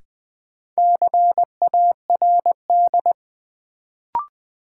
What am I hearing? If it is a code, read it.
Morse code, heard as a keyed tone, about 20 wpm, 700 Hz. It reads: CARD
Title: Sentences from Top 600 Words - New Word Drill - 20wpm